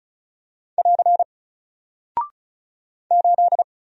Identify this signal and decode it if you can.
Morse code, heard as a keyed tone, about 36 wpm, 700 Hz. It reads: <AR> 8
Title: Single Character (Letter:Number:Punctuation:Prosign) Alt 35wpm